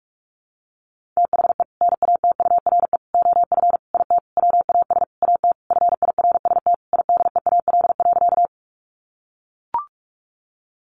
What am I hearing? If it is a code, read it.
Morse code, heard as a keyed tone, about 44 wpm, 700 Hz. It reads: THE NATURE OF IT WAS AT FIRST IDEAL.
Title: Sentences from Top 1400 Words 45wpm